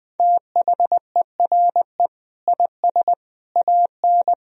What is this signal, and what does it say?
Morse code, heard as a keyed tone, about 20 wpm, 700 Hz. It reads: THERE IS AN
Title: Sentences from Top 1000 Words - Review Part 3 - 20wpm